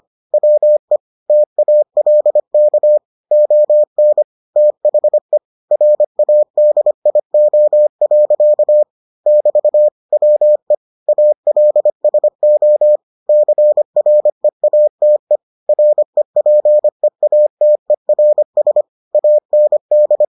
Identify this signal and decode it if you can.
Morse code, heard as a keyed tone, about 25 wpm, 600 Hz. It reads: WE TALK ON THE RADIO. <BT> WE ALSO CREATE REPEATERS AND